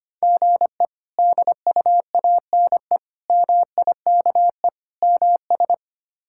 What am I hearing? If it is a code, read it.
Morse code, heard as a keyed tone, about 25 wpm, 700 Hz. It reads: GE DUANE MIKE MS
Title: K1USN SST QSO Practice - 25wpm